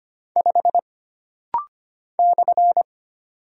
Morse code, heard as a keyed tone, about 25 wpm, 700 Hz. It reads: 5 /
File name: Single Character (Letter:Number:Punctuation:Prosign) - Rapid-Fire - 25wpm